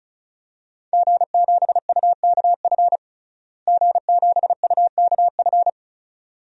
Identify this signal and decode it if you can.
Morse code, heard as a keyed tone, about 34 wpm, 700 Hz. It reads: G7UKF G7UKF
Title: DXCC-100 Call Signs - 5 Characters - Repeated 2x - 35wpm